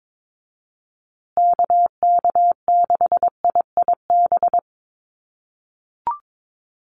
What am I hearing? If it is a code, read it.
Morse code, heard as a keyed tone, about 22 wpm, 700 Hz. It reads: KK6IIB
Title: USA Call Signs Repeated 2x 22wpm